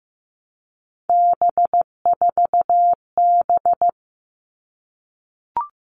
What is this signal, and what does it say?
Morse code, heard as a keyed tone, about 15 wpm, 700 Hz. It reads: B4B
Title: All Cut Numbers - 3-Digit 15wpm